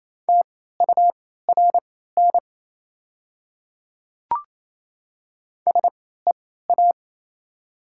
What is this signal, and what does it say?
Morse code, heard as a keyed tone, about 27 wpm, spaced as for 16 wpm, 700 Hz. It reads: TURN SEA
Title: 3x Character Spacing - Top 300 Words - 28wpm